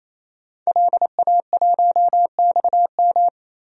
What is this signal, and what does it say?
Morse code, heard as a keyed tone, about 28 wpm, 700 Hz. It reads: LA1XM
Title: DXCC-100 Call Signs - 5 Characters - 28wpm